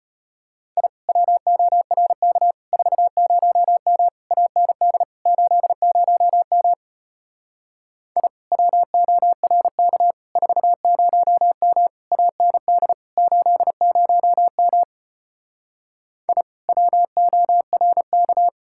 Morse code, heard as a keyed tone, about 33 wpm, 700 Hz. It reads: I WORK 40M AND 80M I WORK 40M AND 80M I WORK